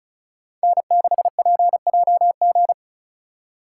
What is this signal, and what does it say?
Morse code, heard as a keyed tone, about 36 wpm, 700 Hz. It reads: N6PJG
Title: USA Call Signs - 5 Characters - 2x Character Spacing - 35wpm